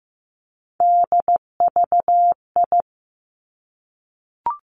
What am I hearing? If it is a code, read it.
Morse code, heard as a keyed tone, about 15 wpm, 700 Hz. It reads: DVI
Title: Three Challenging Letter-Number Combinations 15wpm